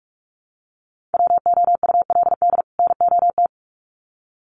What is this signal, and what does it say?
Morse code, heard as a keyed tone, about 44 wpm, 700 Hz. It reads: WOULD NOT
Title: Sets of 2 Words 45wpm